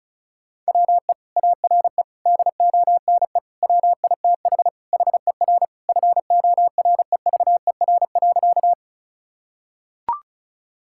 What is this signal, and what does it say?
Morse code, heard as a keyed tone, about 36 wpm, 700 Hz. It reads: WE ARE DONE WITH HER FOREVER.
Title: Sentences from Top 1600 Words 35wpm